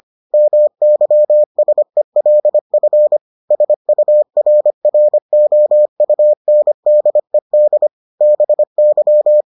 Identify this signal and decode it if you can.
Morse code, heard as a keyed tone, about 25 wpm, 600 Hz. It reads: MYSELF SURROUNDED BY